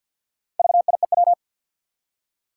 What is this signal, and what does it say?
Morse code, heard as a keyed tone, about 52 wpm, 700 Hz. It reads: VIEW
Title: Four Letter Words - 50wpm